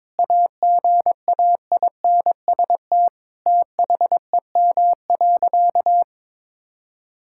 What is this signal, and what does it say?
Morse code, heard as a keyed tone, about 22 wpm, 700 Hz. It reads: AGAINST THEM.
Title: Sentences from Top 1000 Words - Review Part 6 - 22wpm